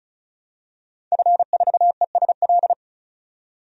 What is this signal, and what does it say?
Morse code, heard as a keyed tone, about 36 wpm, 700 Hz. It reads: F4ESL
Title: DXCC-100 Call Signs - 5 Characters - 35wpm